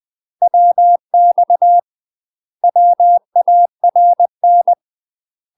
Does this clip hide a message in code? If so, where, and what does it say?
Yes: Morse code from 0.4 to 4.7 s, heard as a keyed tone, about 20 wpm, 700 Hz. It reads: WX WARN